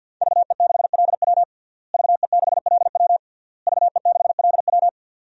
Morse code, heard as a keyed tone, about 46 wpm, 700 Hz. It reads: VE6LW VE6LW VE6LW
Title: DXCC-100 Call Signs - 5 Characters - Repeated 3x - 50wpm